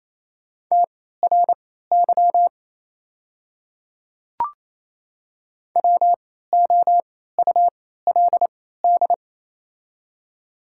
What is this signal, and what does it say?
Morse code, heard as a keyed tone, about 28 wpm, spaced as for 16 wpm, 690 Hz. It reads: TRY WOULD
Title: 3x Character Spacing - Top 200 Words - 28wpm